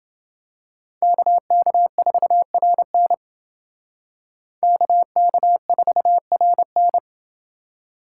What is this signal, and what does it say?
Morse code, heard as a keyed tone, about 27 wpm, 700 Hz. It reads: KK4RN KK4RN